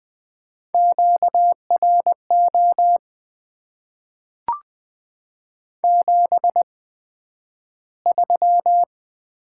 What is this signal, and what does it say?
Morse code, heard as a keyed tone, about 20 wpm, spaced as for 11 wpm, 700 Hz. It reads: QRO 73
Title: Top 90 QSO Elements - 8x Character Spacing - 20wpm